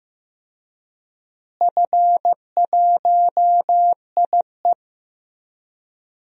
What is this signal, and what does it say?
Morse code, heard as a keyed tone, about 15 wpm, 700 Hz. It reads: F1IE
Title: DXCC-100 Call Signs - 4 Characters - Repeated 3x - 15wpm